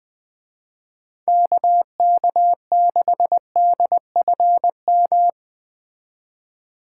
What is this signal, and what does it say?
Morse code, heard as a keyed tone, about 20 wpm, 700 Hz. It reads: KK6DFM
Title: Speed Racing - USA 2x3 Call Signs 17wpm